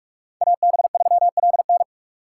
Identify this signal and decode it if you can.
Morse code, heard as a keyed tone, about 58 wpm, 690 Hz. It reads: AB3LN